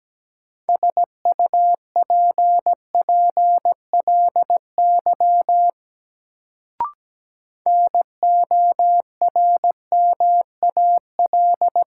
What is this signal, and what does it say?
Morse code, heard as a keyed tone, about 17 wpm, 700 Hz. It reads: SUPPLY NORMAL
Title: Sentences from Top 1200 Words - New Word Drill Alt - 17wpm